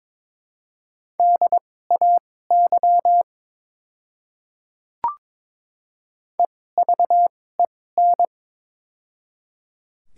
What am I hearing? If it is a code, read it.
Morse code, heard as a keyed tone, about 22 wpm, spaced as for 16 wpm, 700 Hz. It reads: DAY EVEN
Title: 2x Character Spacing - Top 200 Words - 22wpm